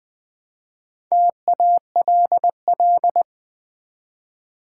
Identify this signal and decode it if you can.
Morse code, heard as a keyed tone, about 20 wpm, 700 Hz. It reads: TALL
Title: Build A Word - 1 Letter At A Time - 4-Letter Words - 20wpm